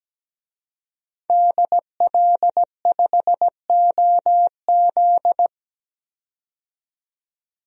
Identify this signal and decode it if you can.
Morse code, heard as a keyed tone, about 17 wpm, 700 Hz. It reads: DL5OZ
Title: DXCC-100 Call Signs - 5 Characters - 17wpm